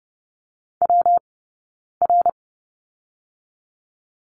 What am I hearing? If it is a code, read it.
Morse code, heard as a keyed tone, about 30 wpm, 700 Hz. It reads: W R
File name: Lesson 19d - Letter w - Two Characters 30wpm